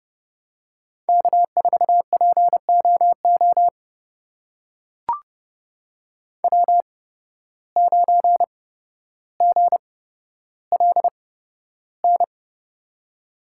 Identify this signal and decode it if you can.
Morse code, heard as a keyed tone, about 31 wpm, spaced as for 13 wpm, 700 Hz. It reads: K4POO W9GLN
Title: USA Call Signs - 5 Characters - 8x Character Spacing - 30wpm